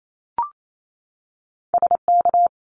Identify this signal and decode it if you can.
Morse code, heard as a keyed tone, about 28 wpm, 700 Hz. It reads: SK
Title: ARRL Sections 28wpm